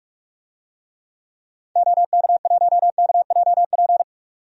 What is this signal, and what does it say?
Morse code, heard as a keyed tone, about 43 wpm, 700 Hz. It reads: OK1KJP
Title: DXCC-100 Call Signs - 6 Characters - 45wpm